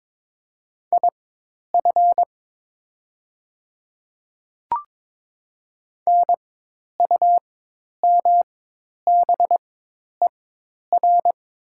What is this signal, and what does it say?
Morse code, heard as a keyed tone, about 22 wpm, spaced as for 10 wpm, 700 Hz. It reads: IF NUMBER
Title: 4x Character Spacing - Top 200 Words - 22wpm